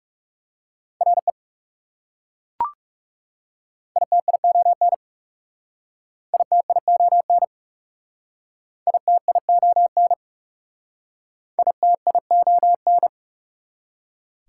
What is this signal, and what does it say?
Morse code, heard as a keyed tone, about 37 wpm, 700 Hz. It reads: RE ITION ITION ITION ITION
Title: Speed-Racing Prefixes and Suffixes 30wpm